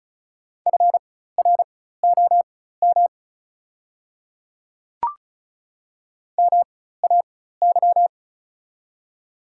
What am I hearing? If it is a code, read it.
Morse code, heard as a keyed tone, about 36 wpm, spaced as for 16 wpm, 700 Hz. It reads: FROM MAY
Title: 4x Character Spacing - Top 500 Words - 35wpm